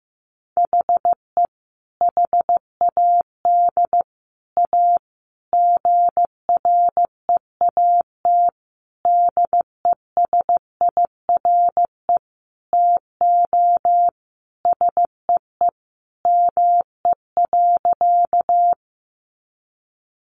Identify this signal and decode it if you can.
Morse code, heard as a keyed tone, about 15 wpm, 700 Hz. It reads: HE HAD A GREAT DESIRE TO SEE ME.
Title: Sentences from Top 1100 Words 15wpm